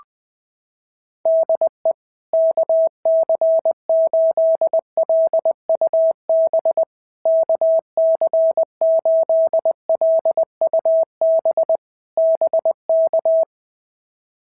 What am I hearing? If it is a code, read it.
Morse code, heard as a keyed tone, about 21 wpm, 650 Hz. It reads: DE KC8LUB KC8LUB BK
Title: POTA QSO Practice - 20wpm